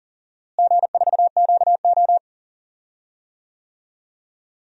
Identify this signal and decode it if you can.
Morse code, heard as a keyed tone, about 38 wpm, 700 Hz. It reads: G4QO
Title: DXCC-100 Call Signs - 4 Characters - Repeated 3x - 40wpm